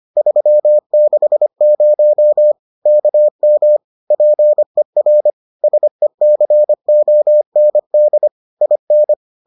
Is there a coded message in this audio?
Yes: Morse code from 0.2 to 9.1 s, heard as a keyed tone, about 25 wpm, 600 Hz. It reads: 360 KM PER SECOND IN